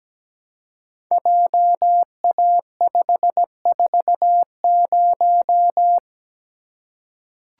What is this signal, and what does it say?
Morse code, heard as a keyed tone, about 17 wpm, 700 Hz. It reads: JA540
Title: Code Groups - 5 Letter:Numbers - 17wpm